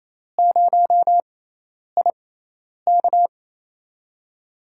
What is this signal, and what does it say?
Morse code, heard as a keyed tone, about 28 wpm, spaced as for 10 wpm, 700 Hz. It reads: 0IK
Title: USA Call Signs - 4 Characters - 6x Character Spacing - 28wpm